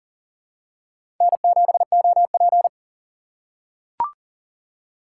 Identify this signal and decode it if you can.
Morse code, heard as a keyed tone, about 40 wpm, 700 Hz. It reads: N7OP
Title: USA Call Signs - 4x Character Spacing - 40wpm